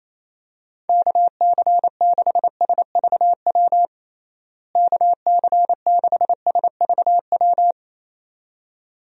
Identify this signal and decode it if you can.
Morse code, heard as a keyed tone, about 27 wpm, 700 Hz. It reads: KC6SVW KC6SVW
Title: USA Call Signs - 6 Characters - Repeated 2x - 28wpm